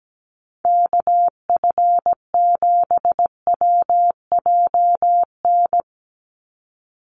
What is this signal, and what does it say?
Morse code, heard as a keyed tone, about 17 wpm, 700 Hz. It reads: KF7WJN
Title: USA Call Signs 17wpm